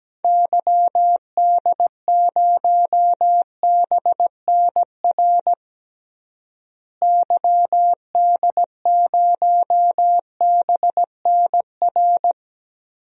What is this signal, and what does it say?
Morse code, heard as a keyed tone, about 17 wpm, 700 Hz. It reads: YD0BNR YD0BNR
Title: DXCC-100 Call Signs - 6 Characters - Repeated 3x - 17wpm